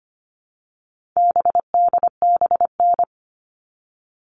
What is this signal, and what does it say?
Morse code, heard as a keyed tone, about 25 wpm, 700 Hz. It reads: BDBN
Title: All Cut Numbers - 4-Digit 25wpm